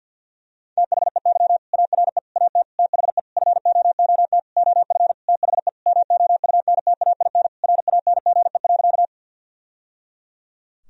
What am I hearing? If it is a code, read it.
Morse code, heard as a keyed tone, about 50 wpm, 700 Hz. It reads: THEY ARE AT THE FOOT OF THE MOUNTAIN RANGE.
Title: Sentences from Top 1000 Words - Review Part 4 - 50wpm